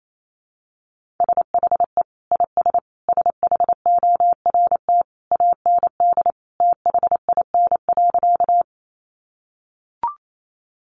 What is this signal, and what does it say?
Morse code, heard as a keyed tone, about 28 wpm, 700 Hz. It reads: SHE IS SHORT AND THIN.
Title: Sentences from Top 1200 Words 28wpm